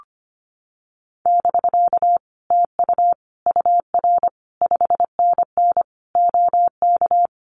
Morse code, heard as a keyed tone, about 25 wpm, 700 Hz. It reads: <BK> TU UR 5NN OK
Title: Common QSO Phrases Encore 25wpm